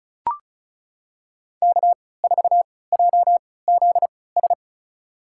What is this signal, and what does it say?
Morse code, heard as a keyed tone, about 36 wpm, spaced as for 20 wpm, 700 Hz. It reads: K4JZS
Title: USA Call Signs - 5 Characters - 3x Character Spacing - 35wpm